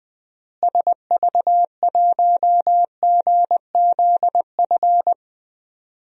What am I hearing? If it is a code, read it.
Morse code, heard as a keyed tone, about 20 wpm, 700 Hz. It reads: SV1GZF
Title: DXCC-100 Call Signs - 6 Characters - Repeated 3x - 20wpm